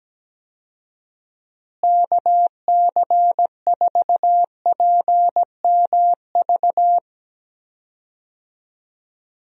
Morse code, heard as a keyed tone, about 17 wpm, 700 Hz. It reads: KC4PMV
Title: USA Call Signs - 6 Characters - Repeated 3x - 17wpm